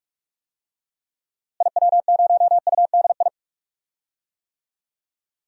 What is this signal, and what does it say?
Morse code, heard as a keyed tone, about 47 wpm, 700 Hz. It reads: IW0UDI
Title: DXCC-100 Call Signs - 6 Characters - Repeated 2x - 45wpm